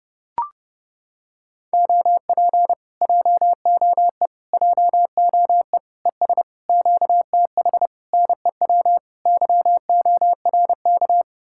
Morse code, heard as a keyed tone, about 31 wpm, 700 Hz. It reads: OP JOE JOE ES QTH NEW YORK